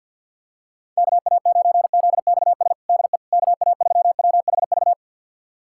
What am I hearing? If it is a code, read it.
Morse code, heard as a keyed tone, about 51 wpm, 690 Hz. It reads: KA9ZXS DE KA3WHV
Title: USA Callsign DE Callsign - 6 Characters - 50wpm